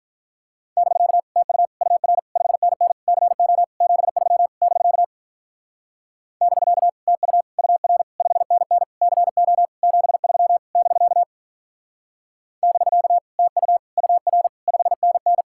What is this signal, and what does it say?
Morse code, heard as a keyed tone, about 46 wpm, 700 Hz. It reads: <BK> TU UR 5NN CO 73 <BK> <BK> TU UR 5NN CO 73 <BK> <BK> TU UR 5NN